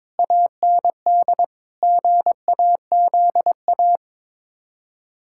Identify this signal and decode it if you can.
Morse code, heard as a keyed tone, about 22 wpm, 700 Hz. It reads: AND GAZA
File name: practice / 22wpm / Build A Phrase - 5-Word Phrases - 22wpm